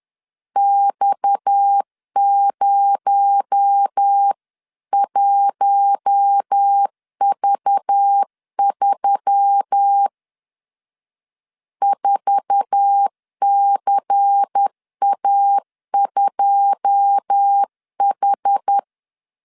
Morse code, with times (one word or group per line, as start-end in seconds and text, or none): 0.6-10.1 X01V3
11.8-18.8 4CA2H